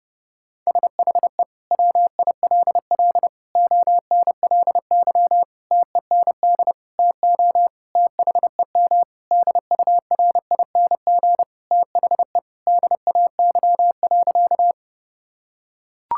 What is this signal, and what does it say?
Morse code, heard as a keyed tone, about 32 wpm, 700 Hz. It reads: SHE WILL ONLY TEND TO THEM DURING THE DAY.
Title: Sentences from Top 1000 Words - Review Part 5 - 30wpm